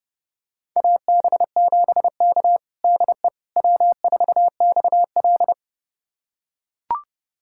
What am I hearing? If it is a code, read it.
Morse code, heard as a keyed tone, about 30 wpm, 700 Hz. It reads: AB7K DE W4XL
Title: USA Callsign DE Callsign - 4 Characters - 30wpm